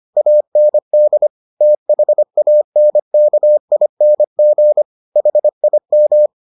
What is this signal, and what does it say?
Morse code, heard as a keyed tone, about 25 wpm, 600 Hz. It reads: AND THANKING HIM